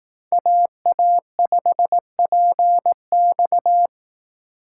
Morse code, heard as a keyed tone, about 18 wpm, 700 Hz. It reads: AA5PX
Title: Speed Racing - USA 2x2 Call Signs 15wpm